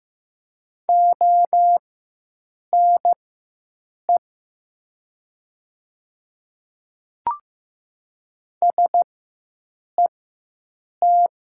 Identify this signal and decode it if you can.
Morse code, heard as a keyed tone, about 15 wpm, spaced as for 7 wpm, 700 Hz. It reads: ONE SET
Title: 4x Character Spacing - Top 200 Words - 15wpm